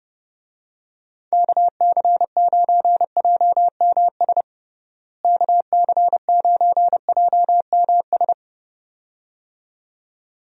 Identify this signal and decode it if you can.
Morse code, heard as a keyed tone, about 30 wpm, 700 Hz. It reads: KC9JMS KC9JMS
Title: USA Call Signs - 6 Characters - Repeated 2x - 30wpm